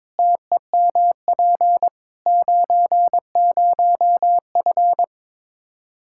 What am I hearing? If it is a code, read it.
Morse code, heard as a keyed tone, about 22 wpm, 700 Hz. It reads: TEMP 90F
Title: Practice QSO - Exchange 2 - Info - 22wpm